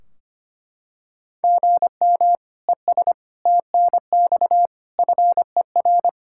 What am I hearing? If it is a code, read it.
Morse code, heard as a keyed tone, about 25 wpm, 700 Hz. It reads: GM ES TNX FER
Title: Practice QSO - Exchange 1 - Beginning - 25wpm